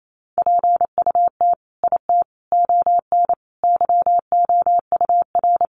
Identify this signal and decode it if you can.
Morse code, heard as a keyed tone, about 28 wpm, 700 Hz. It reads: PUT IT ON YOUR
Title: Sentences from Top 1400 Words 28wpm